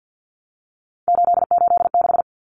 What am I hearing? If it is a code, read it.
Morse code, heard as a keyed tone, about 50 wpm, 700 Hz. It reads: 886
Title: Numbers - 3-Digit 50wpm